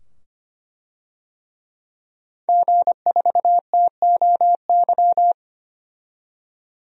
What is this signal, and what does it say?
Morse code, heard as a keyed tone, about 25 wpm, 700 Hz. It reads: G4TOY
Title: DXCC-100 Call Signs - 5 Characters - 25wpm